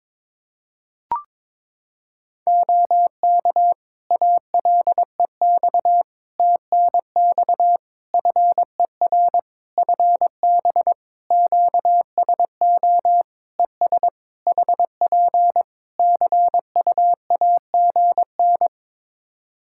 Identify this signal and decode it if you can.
Morse code, heard as a keyed tone, about 22 wpm, 700 Hz. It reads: OK ALEX TNX FER FB QSO ES HP CUAGN
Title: Practice QSO - Exchange 4 - Ending - 22wpm